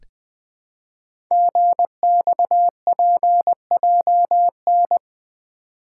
Morse code, heard as a keyed tone, about 20 wpm, 700 Hz. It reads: GXPJN
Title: Code Groups - 5 Letter:Numbers - 20wpm